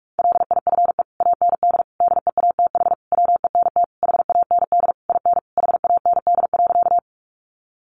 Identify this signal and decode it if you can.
Morse code, heard as a keyed tone, about 46 wpm, 700 Hz. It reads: LIFE AND DEATH WENT HAND IN HAND.
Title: Sentences from Top 1100 Words 45wpm